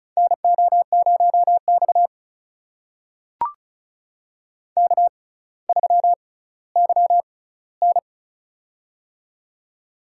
Morse code, heard as a keyed tone, about 36 wpm, spaced as for 18 wpm, 700 Hz. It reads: NO0X K3YN
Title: USA Call Signs - 4 Characters - 6x Character Spacing - 35wpm